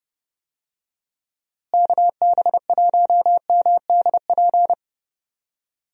Morse code, heard as a keyed tone, about 30 wpm, 700 Hz. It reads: KB1MDP
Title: USA Call Signs - 6 Characters - Repeated 2x - 30wpm